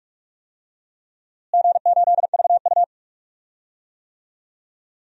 Morse code, heard as a keyed tone, about 45 wpm, 700 Hz. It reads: G8VU
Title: DXCC-100 Call Signs - 4 Characters - Repeated 3x - 45wpm